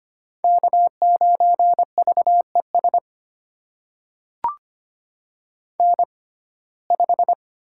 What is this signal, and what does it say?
Morse code, heard as a keyed tone, about 25 wpm, spaced as for 18 wpm, 700 Hz. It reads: K9VES N5
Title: USA Call Signs - 5 Characters - 6x Character Spacing - 25wpm